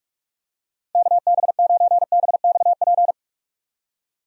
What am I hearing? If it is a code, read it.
Morse code, heard as a keyed tone, about 46 wpm, 700 Hz. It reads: KB9BXP
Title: USA Call Signs - 6 Characters - 45wpm